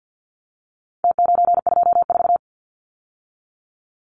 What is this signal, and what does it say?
Morse code, heard as a keyed tone, about 48 wpm, 700 Hz. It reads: T924
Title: Numbers - Contest Style - 4-Digit 50wpm